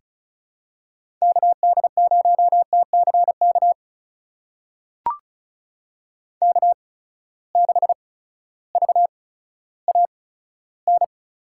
Morse code, heard as a keyed tone, about 36 wpm, spaced as for 16 wpm, 700 Hz. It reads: KD0TCK K6VAN
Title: USA Call Signs - 8x Character Spacing - 35wpm